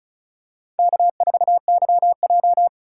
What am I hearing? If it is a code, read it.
Morse code, heard as a keyed tone, about 36 wpm, 690 Hz. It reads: K4YJ
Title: USA Call Signs - 4 Characters - 3x Character Spacing - 35wpm